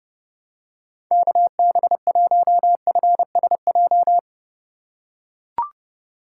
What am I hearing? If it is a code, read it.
Morse code, heard as a keyed tone, about 30 wpm, 700 Hz. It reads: KB1FSJ
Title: USA Call Signs - 2x Character Spacing - 30wpm